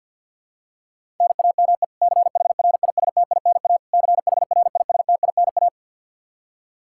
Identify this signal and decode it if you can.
Morse code, heard as a keyed tone, about 49 wpm, 700 Hz. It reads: NAME CHRISTINA CHRISTINA
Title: K1USN SST QSO Practice 50wpm